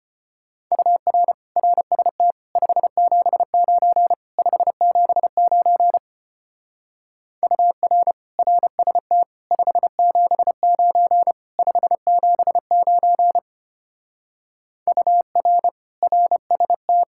Morse code, heard as a keyed tone, about 31 wpm, 700 Hz. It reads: UR RST 579 579 UR RST 579 579 UR RST
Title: Speed-Racing Common QSO Phrases - 25wpm